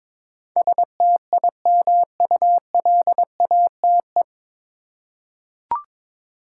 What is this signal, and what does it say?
Morse code, heard as a keyed tone, about 22 wpm, 700 Hz. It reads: STIMULATE
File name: 9 Letter Words - First 3 Letters Given - 22wpm